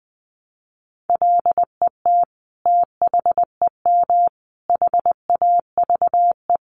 Morse code, heard as a keyed tone, about 20 wpm, 700 Hz. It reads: LET THEM HAVE
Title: Sentences from Top 300 Words 20wpm